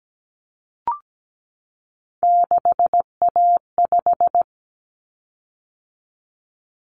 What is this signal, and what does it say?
Morse code, heard as a keyed tone, about 17 wpm, 700 Hz. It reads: 6A5
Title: Common Cut Numbers - 3-Digit 17wpm